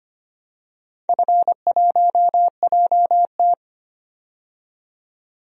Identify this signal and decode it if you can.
Morse code, heard as a keyed tone, about 25 wpm, 700 Hz. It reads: F1JT
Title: DXCC-100 Call Signs - 4 Characters - 25wpm